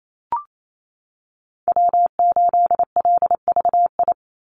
Morse code, heard as a keyed tone, about 28 wpm, 700 Hz. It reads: W8LVI
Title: Call Signs Repeated 2x - 28wpm